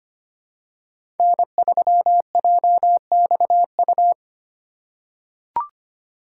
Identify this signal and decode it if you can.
Morse code, heard as a keyed tone, about 25 wpm, 700 Hz. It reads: N3JXU
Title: USA Call Signs - 5 Characters - 3x Character Spacing - 25wpm